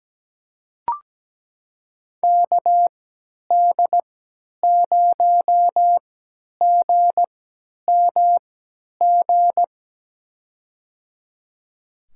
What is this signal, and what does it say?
Morse code, heard as a keyed tone, about 17 wpm, spaced as for 10 wpm, 700 Hz. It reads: KD0GMG
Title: USA Call Signs - 3x Character Spacing - 17wpm